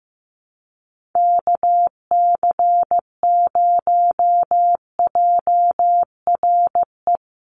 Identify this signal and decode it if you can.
Morse code, heard as a keyed tone, about 15 wpm, 700 Hz. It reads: KC0JRE
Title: USA Call Signs 15wpm